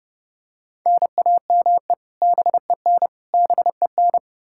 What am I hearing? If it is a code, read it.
Morse code, heard as a keyed tone, about 32 wpm, 700 Hz. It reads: NAME BEN BEN
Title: K1USN SST QSO Practice 30wpm